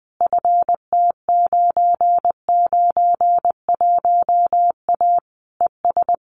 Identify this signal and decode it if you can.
Morse code, heard as a keyed tone, about 20 wpm, 700 Hz. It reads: FT991A ES
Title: Common QSO Phrases Encore 20wpm